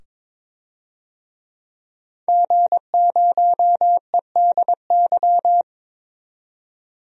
Morse code, heard as a keyed tone, about 23 wpm, 700 Hz. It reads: G0EDY
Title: DXCC-100 Call Signs - 5 Characters - 22wpm